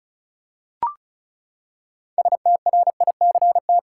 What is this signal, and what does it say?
Morse code, heard as a keyed tone, about 36 wpm, 700 Hz. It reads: STRICT
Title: Six Letter Words - 35wpm